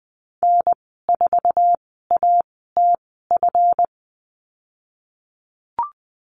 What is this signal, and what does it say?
Morse code, heard as a keyed tone, about 20 wpm, spaced as for 15 wpm, 700 Hz. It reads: N4ATF
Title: 179 Lesson 29g - Slash - Call Signs 20wpm